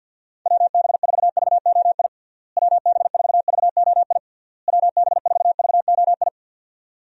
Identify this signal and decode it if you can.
Morse code, heard as a keyed tone, about 52 wpm, 700 Hz. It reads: WB4VOI WB4VOI WB4VOI
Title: USA Call Signs - 6 Characters - Repeated 3x - 50wpm